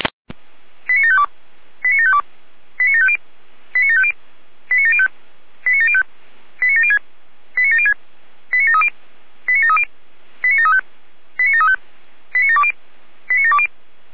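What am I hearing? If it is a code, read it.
Five-tone selective call (ZVEI), heard as quick runs of five tones, tones of about 70 ms. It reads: 89752 89752 89750 89750 89705 89705 89706 89706 89730 89730 89735 89735 89720 89720
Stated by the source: ZVEI five-tone call 89490 received by SDR, channel 455.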